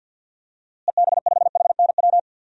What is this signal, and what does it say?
Morse code, heard as a keyed tone, about 46 wpm, 700 Hz. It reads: EB5HNW
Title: DXCC-100 Call Signs - 6 Characters - Repeated 2x - 50wpm